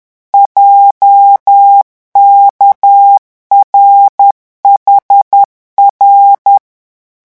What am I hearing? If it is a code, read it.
Morse code, heard as a keyed tone, about 10 wpm, 800 Hz. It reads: JKRHR
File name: morse mp3 / revision11 lettres vitesse 10